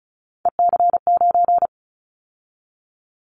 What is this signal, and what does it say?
Morse code, heard as a keyed tone, about 34 wpm, 700 Hz. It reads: EC9
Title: Callsign Prefixes for Top 100 DXCC - 35wpm